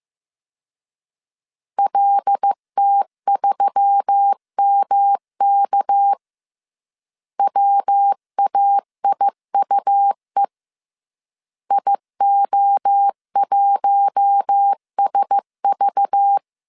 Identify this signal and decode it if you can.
Morse code, heard as a keyed tone, about 15 wpm, 800 Hz. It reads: LT3MK WAIUE IO1SV
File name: CW / DET / 1 6 1500C S C 14wpm 2951